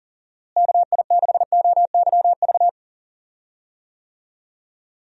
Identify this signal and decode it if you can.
Morse code, heard as a keyed tone, about 40 wpm, 700 Hz. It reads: KI6OYV
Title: USA Call Signs - 6 Characters - Repeated 2x - 40wpm